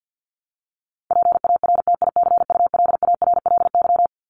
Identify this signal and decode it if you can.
Morse code, heard as a keyed tone, about 50 wpm, 700 Hz. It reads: PARTICULARLY
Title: LY Suffix - Top 100 Words with Base Words 50wpm